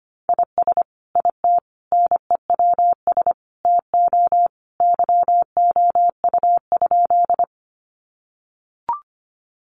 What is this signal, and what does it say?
Morse code, heard as a keyed tone, about 25 wpm, 700 Hz. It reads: IS IT NEWS TO YOU?
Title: Sentences from Top 800 Words 25wpm